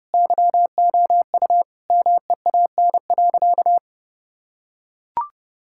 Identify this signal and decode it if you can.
Morse code, heard as a keyed tone, about 30 wpm, 700 Hz. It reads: YOU MEAN.
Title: Sentences from Top 1000 Words - Review Part 6 - 30wpm